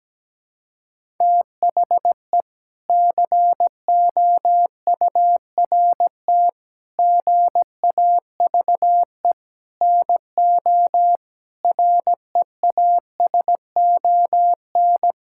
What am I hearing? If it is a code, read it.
Morse code, heard as a keyed tone, about 17 wpm, 700 Hz. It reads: THE COURT GAVE NO REASON
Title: Sentences from Top 1600 Words 17wpm